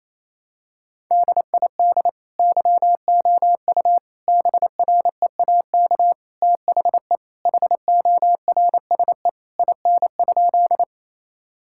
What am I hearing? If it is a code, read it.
Morse code, heard as a keyed tone, about 28 wpm, 700 Hz. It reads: DID YOU BREAK THE HORSE IN?
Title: Sentences from Top 1000 Words - Review Part 6 - 28wpm